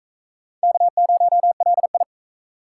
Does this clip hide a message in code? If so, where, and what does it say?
Yes: Morse code from 0.6 to 2.0 s, heard as a keyed tone, about 37 wpm, 700 Hz. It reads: K0LI